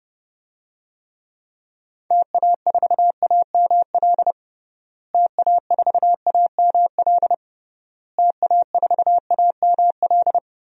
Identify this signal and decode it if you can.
Morse code, heard as a keyed tone, about 32 wpm, 700 Hz. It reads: TA4AML TA4AML TA4AML
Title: DXCC-100 Call Signs - 6 Characters - Repeated 3x - 30wpm